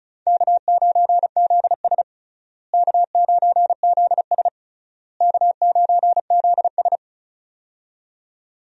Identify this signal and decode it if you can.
Morse code, heard as a keyed tone, about 34 wpm, 700 Hz. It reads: K9ZS K9ZS K9ZS
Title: USA Call Signs - 4 Characters - Repeated 3x - 35wpm